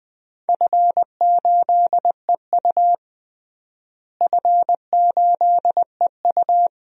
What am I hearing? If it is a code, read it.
Morse code, heard as a keyed tone, about 20 wpm, 700 Hz. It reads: F8EU F8EU
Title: DXCC-100 Call Signs - 4 Characters - Repeated 2x - 20wpm